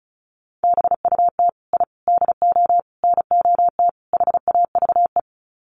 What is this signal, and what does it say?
Morse code, heard as a keyed tone, about 34 wpm, 700 Hz. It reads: BUT I DO NOT HAVE
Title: Sets of 5 Words Encore 35wpm